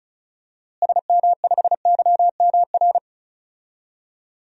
Morse code, heard as a keyed tone, about 36 wpm, 700 Hz. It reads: SM5YMR
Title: DXCC-100 Call Signs - 6 Characters - Repeated 3x - 35wpm